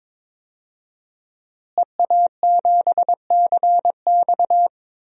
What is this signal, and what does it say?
Morse code, heard as a keyed tone, about 22 wpm, 700 Hz. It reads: EA7CX
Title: DXCC-100 Call Signs - 5 Characters - Repeated 2x - 22wpm